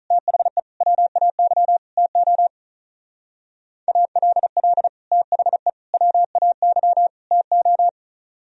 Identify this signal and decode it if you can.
Morse code, heard as a keyed tone, about 35 wpm, 700 Hz. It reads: THE WAY TO ALL THE WAY TO
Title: Speed-Racing - Sets of 4 Words 35wpm